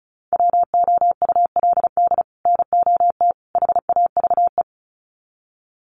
Morse code, heard as a keyed tone, about 34 wpm, 700 Hz. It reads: WOULD NOT HAVE
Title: Sets of 3 Words 35wpm